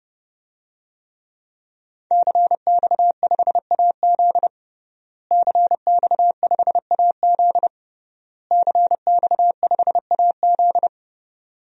Morse code, heard as a keyed tone, about 30 wpm, 700 Hz. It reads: CX5AZ CX5AZ CX5AZ
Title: DXCC-100 Call Signs - 5 Characters - Repeated 3x - 30wpm